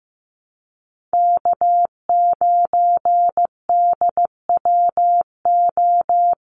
Morse code, heard as a keyed tone, about 16 wpm, 700 Hz. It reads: K9DWO
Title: Common QSO Phrases Encore 15wpm